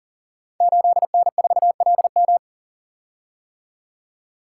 Morse code, heard as a keyed tone, about 38 wpm, 700 Hz. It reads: 8N4LM
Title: Code Groups - 5 Letter:Numbers - 40wpm